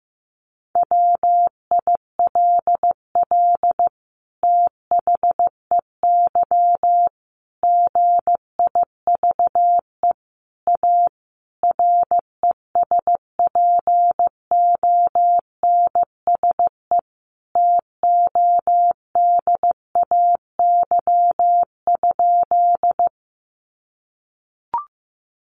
Morse code, heard as a keyed tone, about 15 wpm, 700 Hz. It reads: WILL THEY GIVE A RESPONSE TODAY?
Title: Sentences from Top 1400 Words 15wpm